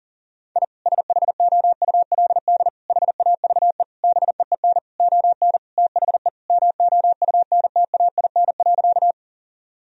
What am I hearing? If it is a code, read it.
Morse code, heard as a keyed tone, about 40 wpm, 700 Hz. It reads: I SHOULD HAVE BEEN ON THE MOUNTAIN.
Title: Sentences from Top 1000 Words - Review Part 1 - 40wpm